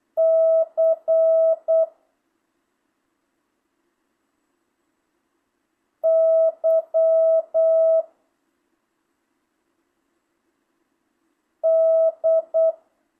Morse code, heard as a keyed tone, about 8 wpm, 650 Hz. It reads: C Y D
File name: jh8cbh / 2025cw